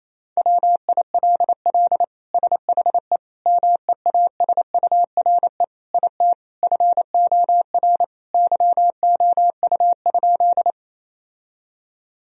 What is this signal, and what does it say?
Morse code, heard as a keyed tone, about 28 wpm, 700 Hz. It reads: WILL SHE MEASURE IT FOR YOU?
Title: Sentences from Top 1000 Words - Review Part 2 - 28wpm